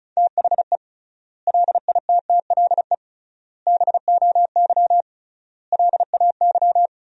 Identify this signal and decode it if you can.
Morse code, heard as a keyed tone, about 36 wpm, 700 Hz. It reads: THE LITTLE BOY LAY
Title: 3x Word-Spacing - Sentences from Top 500 Words - 35wpm